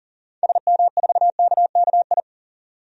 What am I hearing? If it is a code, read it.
Morse code, heard as a keyed tone, about 40 wpm, 700 Hz. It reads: SM4KKI
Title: DXCC-100 Call Signs - 6 Characters - 40wpm